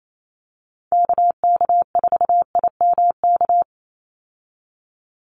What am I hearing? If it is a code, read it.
Morse code, heard as a keyed tone, about 28 wpm, 700 Hz. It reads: KK4IMK
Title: USA Call Signs 28wpm